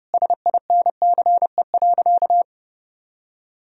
Morse code, heard as a keyed tone, about 32 wpm, 700 Hz. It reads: SINCE.
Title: Sentences from Top 1000 Words - Review Part 3 - 30wpm